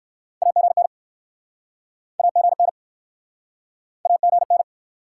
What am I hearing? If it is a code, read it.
Morse code, heard as a keyed tone, about 66 wpm, 700 Hz. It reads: AGN AGN AGN